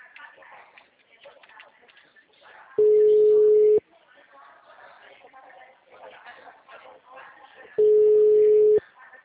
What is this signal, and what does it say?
Ringback tone (Europe, 425 Hz), with speech.
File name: out- -2004- - - .28343